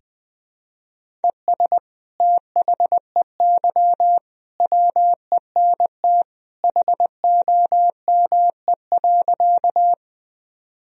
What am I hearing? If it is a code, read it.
Morse code, heard as a keyed tone, about 20 wpm, 700 Hz. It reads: ES THEY WENT HOME.
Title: Sentences from Top 200 Words and Abbreviations 20wpm